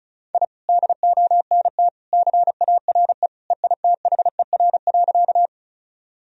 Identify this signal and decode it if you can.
Morse code, heard as a keyed tone, about 34 wpm, 700 Hz. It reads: I DONT CARE EITHER.
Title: Sentences from Top 1000 Words - Review Part 4 - 35wpm